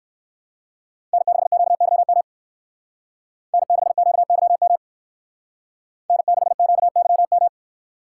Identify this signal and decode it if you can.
Morse code, heard as a keyed tone, about 60 wpm, 700 Hz. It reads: N6QYM N6QYM N6QYM